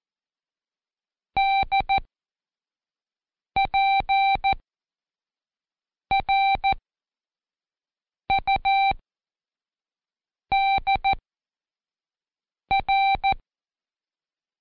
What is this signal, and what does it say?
Morse code, heard as a keyed tone, about 14 wpm, 780 Hz. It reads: D P R U D R
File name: Code Quick / AudioDownld1 / Seg26-Day3-Review UDPR